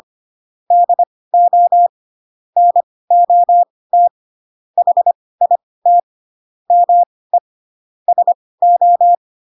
Morse code, heard as a keyed tone, about 25 wpm, spaced as for 18 wpm, 700 Hz. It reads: DO NOT HIT ME SO
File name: Pinocchio / Pinocchio ch1 12 25 ik1hge